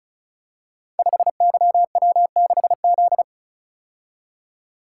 Morse code, heard as a keyed tone, about 36 wpm, 700 Hz. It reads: 5YW6Z
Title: Code Groups - 5 Letter:Numbers - 35wpm